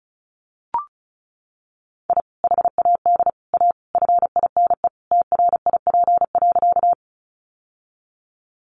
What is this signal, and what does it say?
Morse code, heard as a keyed tone, about 34 wpm, 700 Hz. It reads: I HAD A FINE TRIP.
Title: Sentences from Top 1300 Words 35wpm